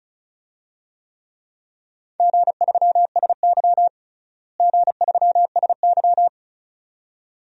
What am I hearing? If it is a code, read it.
Morse code, heard as a keyed tone, about 36 wpm, 700 Hz. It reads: G3SY G3SY
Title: DXCC-100 Call Signs - 4 Characters - Repeated 2x - 35wpm